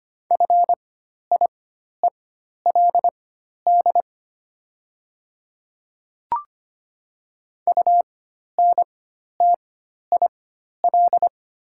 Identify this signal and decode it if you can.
Morse code, heard as a keyed tone, about 25 wpm, spaced as for 12 wpm, 700 Hz. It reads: FIELD UNTIL
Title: 4x Character Spacing - Top 500 Words - 25wpm